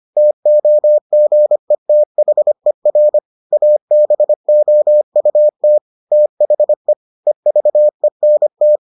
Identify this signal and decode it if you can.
Morse code, heard as a keyed tone, about 25 wpm, 600 Hz. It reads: TOGETHER ABOUT THE EVENT